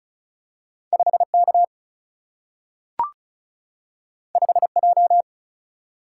Morse code, heard as a keyed tone, about 36 wpm, 700 Hz. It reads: 5K 5J
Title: Single-Letter Single-Number 35wpm